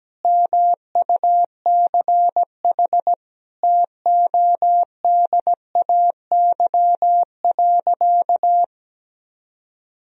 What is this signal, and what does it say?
Morse code, heard as a keyed tone, about 17 wpm, 700 Hz. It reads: MUCH TODAY.
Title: Sentences from Top 1000 Words - Review Part 4 - 17wpm